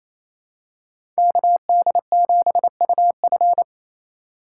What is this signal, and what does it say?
Morse code, heard as a keyed tone, about 28 wpm, 700 Hz. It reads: KD7UF
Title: USA Call Signs - 2x Character Spacing - 28wpm